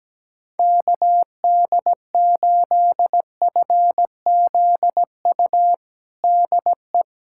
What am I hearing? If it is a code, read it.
Morse code, heard as a keyed tone, about 17 wpm, 700 Hz. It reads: KD8FZU DE
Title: USA Callsign DE Callsign 17wpm